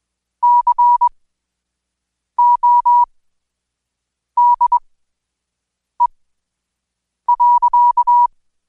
Morse code, heard as a keyed tone, about 21 wpm, spaced as for 6 wpm, 970 Hz. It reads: CODE.
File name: Code Quick/AudioDownld2/Seg109-Day14-DayDrill 14